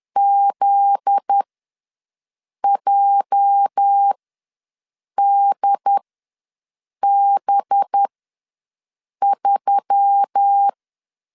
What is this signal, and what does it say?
Morse code, heard as a keyed tone, about 10 wpm, spaced as for 6 wpm, 800 Hz. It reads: ZJDB3
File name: LEZIONI / 8 YJ45 / 8 YJ ws 10wpm 4043